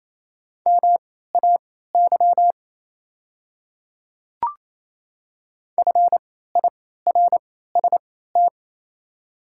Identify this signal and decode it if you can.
Morse code, heard as a keyed tone, about 28 wpm, spaced as for 16 wpm, 690 Hz. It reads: MAY FIRST
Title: 3x Character Spacing - Top 100 Words - 28wpm